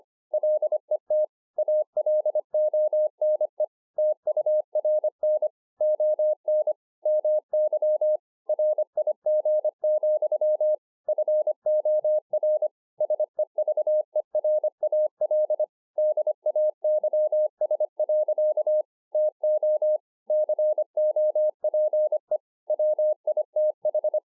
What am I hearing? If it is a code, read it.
Morse code, heard as a keyed tone, about 25 wpm, 610 Hz. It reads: LET ALONE TURN ON MY RIG, FOR SEVERAL DAYS. TO COPE WITH
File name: 2011-04-mp3 / THIS IS NOT MY SHACK 2 25WPM